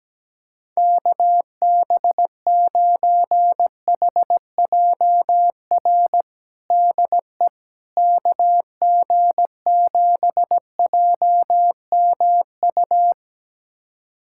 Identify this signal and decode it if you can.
Morse code, heard as a keyed tone, about 17 wpm, 700 Hz. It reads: KB9HJR DE KG7JMU
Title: USA Callsign DE Callsign - 6 Characters - 17wpm